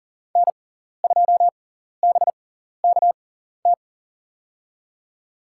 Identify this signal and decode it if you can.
Morse code, heard as a keyed tone, about 40 wpm, spaced as for 14 wpm, 700 Hz. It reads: N2BKT
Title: USA Call Signs - 6x Character Spacing - 40wpm